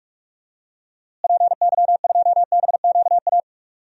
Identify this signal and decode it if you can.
Morse code, heard as a keyed tone, about 43 wpm, 700 Hz. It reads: PY2BQA
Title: DXCC-100 Call Signs - 6 Characters - Repeated 3x - 45wpm